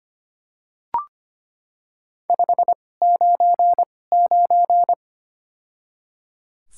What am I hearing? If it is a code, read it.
Morse code, heard as a keyed tone, about 25 wpm, spaced as for 18 wpm, 700 Hz. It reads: 599
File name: Top 90 QSO Elements - 2x Character Spacing - 25wpm